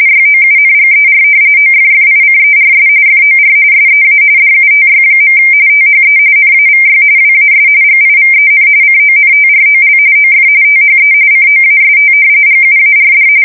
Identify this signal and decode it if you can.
RTTY, about 45.45 baud. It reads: YOUNGER OFFICERS FOR SEA DUTY. THE NAVY WAS GIVEN CONTROL OF THE COAST GUARD
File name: fsk170 / usnavy4